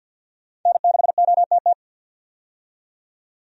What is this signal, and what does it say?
Morse code, heard as a keyed tone, about 46 wpm, 700 Hz. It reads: N6OTT
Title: USA Call Signs - 5 Characters - 50wpm